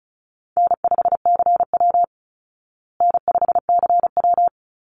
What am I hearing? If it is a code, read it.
Morse code, heard as a keyed tone, about 34 wpm, 700 Hz. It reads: N5CW N5CW
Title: Easy Call Signs Repeated 3x - 35wpm